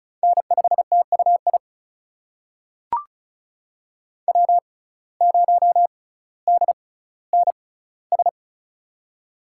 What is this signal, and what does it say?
Morse code, heard as a keyed tone, about 36 wpm, spaced as for 18 wpm, 700 Hz. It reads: N5TUI W0DNS
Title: USA Call Signs - 5 Characters - 6x Character Spacing - 35wpm